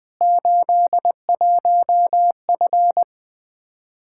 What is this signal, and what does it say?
Morse code, heard as a keyed tone, about 20 wpm, 700 Hz. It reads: 81F
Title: Practice QSO - Exchange 2 - Info - 20wpm